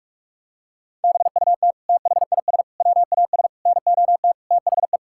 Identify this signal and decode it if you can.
Morse code, heard as a keyed tone, about 46 wpm, 700 Hz. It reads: BUT THIS WAS NOT THE
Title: Sentences from Top 1000 Words - Review Part 6 - 45wpm